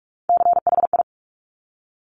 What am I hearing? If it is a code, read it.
Morse code, heard as a keyed tone, about 46 wpm, 700 Hz. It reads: CHI
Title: Top Three Letter Combinations in Words 45wpm